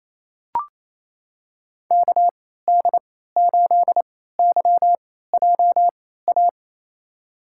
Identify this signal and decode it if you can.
Morse code, heard as a keyed tone, about 28 wpm, spaced as for 16 wpm, 700 Hz. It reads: KD8YJA
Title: USA Call Signs - 6 Characters - 3x Character Spacing - 28wpm